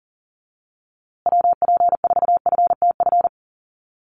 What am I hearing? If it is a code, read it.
Morse code, heard as a keyed tone, about 40 wpm, 700 Hz. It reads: WP4FTF
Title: USA Call Signs Repeated 3x 40wpm